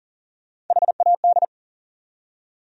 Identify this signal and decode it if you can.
Morse code, heard as a keyed tone, about 54 wpm, spaced as for 47 wpm, 700 Hz. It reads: HAD
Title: Speed-Racing Top 100 Words 40wpm